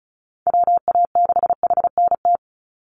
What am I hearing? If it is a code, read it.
Morse code, heard as a keyed tone, about 34 wpm, 700 Hz. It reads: WA6HNT
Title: USA Call Signs 35wpm